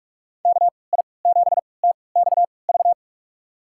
Morse code, heard as a keyed tone, about 46 wpm, spaced as for 26 wpm, 700 Hz. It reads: KI7TXV
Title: USA Call Signs - 6 Characters - 3x Character Spacing - 45wpm